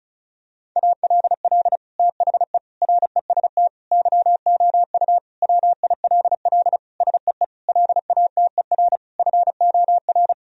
Morse code, heard as a keyed tone, about 36 wpm, 700 Hz. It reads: ALL THE REST YOU WILL SEE LATER FOR
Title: Sentences from Top 1000 Words - Review Part 4 - 35wpm